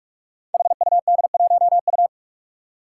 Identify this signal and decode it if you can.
Morse code, heard as a keyed tone, about 44 wpm, 700 Hz. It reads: HUD1U
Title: Five Challenging Letter-Number Combinations 45wpm